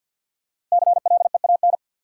Morse code, heard as a keyed tone, about 52 wpm, 700 Hz. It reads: CLEAN
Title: Sentences from Top 1100 Words - New Word Drill Alt - 50wpm